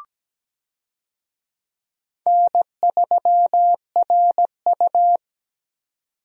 Morse code, heard as a keyed tone, about 17 wpm, 700 Hz. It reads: N3RU
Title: USA Call Signs - 4 Characters - 17wpm